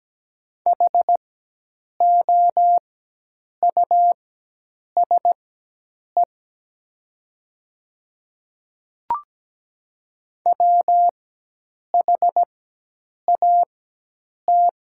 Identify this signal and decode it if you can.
Morse code, heard as a keyed tone, about 17 wpm, spaced as for 8 wpm, 700 Hz. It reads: HOUSE WHAT
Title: 4x Character Spacing - Top 200 Words - 17wpm